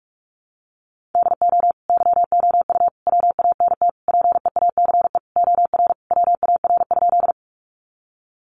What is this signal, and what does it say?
Morse code, heard as a keyed tone, about 44 wpm, 700 Hz. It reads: DO YOU WANT PEACE OR WAR?
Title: Sentences from Top 1500 Words 45wpm